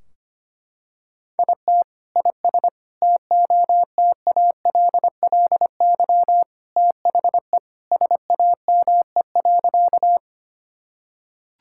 Morse code, heard as a keyed tone, about 25 wpm, 700 Hz. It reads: IT IS TOTALLY THE SAME.
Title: Sentences from Top 1600 Words 25wpm